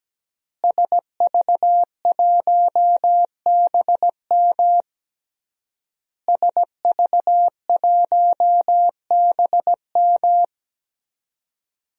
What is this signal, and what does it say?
Morse code, heard as a keyed tone, about 17 wpm, 700 Hz. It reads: SV1BM SV1BM
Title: DXCC-100 Call Signs - 5 Characters - Repeated 3x - 17wpm